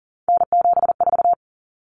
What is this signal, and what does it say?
Morse code, heard as a keyed tone, about 40 wpm, 700 Hz. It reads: N74
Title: Common Cut Numbers - 3-Digit 40wpm